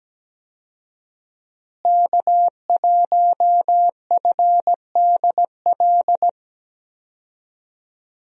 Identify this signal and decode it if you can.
Morse code, heard as a keyed tone, about 17 wpm, 700 Hz. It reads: K1FDL
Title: USA Call Signs - 5 Characters - Repeated 2x - 17wpm